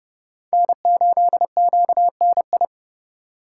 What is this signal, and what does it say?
Morse code, heard as a keyed tone, about 36 wpm, 700 Hz. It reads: N8QNI
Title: Speed Racing - USA 1x3 Call Signs 30wpm